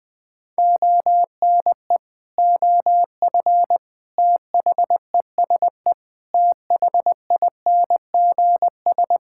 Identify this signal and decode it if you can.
Morse code, heard as a keyed tone, about 21 wpm, 700 Hz. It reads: ONE OF THESE THINGS
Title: Sets of 8 Words - 20wpm